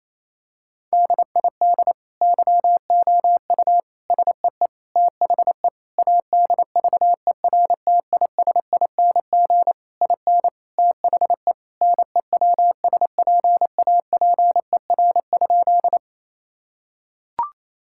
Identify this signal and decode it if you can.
Morse code, heard as a keyed tone, about 28 wpm, 700 Hz. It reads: DID YOU SEE THE ADVERTISING IN THE NEWSPAPER?
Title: Sentences from Top 1000 Words - Review Part 6 - 28wpm